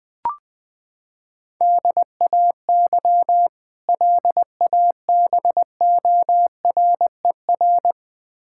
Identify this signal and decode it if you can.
Morse code, heard as a keyed tone, about 21 wpm, 700 Hz. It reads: DAY LABORER
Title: K1USN SST QSO Practice 20wpm